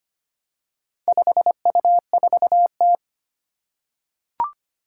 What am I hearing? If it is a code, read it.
Morse code, heard as a keyed tone, about 25 wpm, 700 Hz. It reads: 5U4T
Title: Four Challenging Letter-Number Combinations 25wpm